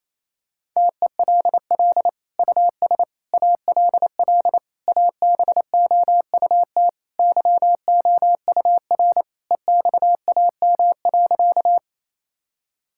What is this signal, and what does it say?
Morse code, heard as a keyed tone, about 28 wpm, 700 Hz. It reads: TELL US ALL ABOUT YOUR EXAM.
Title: Sentences from Top 1000 Words - Review Part 5 - 28wpm